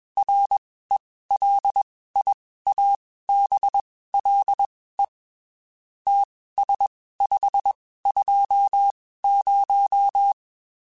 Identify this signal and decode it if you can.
Morse code, heard as a keyed tone, about 21 wpm, spaced as for 15 wpm, 800 Hz. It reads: RELIABLE TS520
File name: QSOs / QSOs-mp3s / WD0FBT 1@20